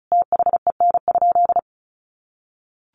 Morse code, heard as a keyed tone, about 34 wpm, 700 Hz. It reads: THEN?
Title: Sentences from Top 100 Words 35wpm